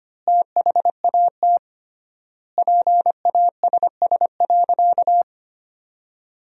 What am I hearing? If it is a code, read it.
Morse code, heard as a keyed tone, about 25 wpm, 700 Hz. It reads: THAT PASS.
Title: 3x Word-Spacing - Sentences from Top 400 Words - 25wpm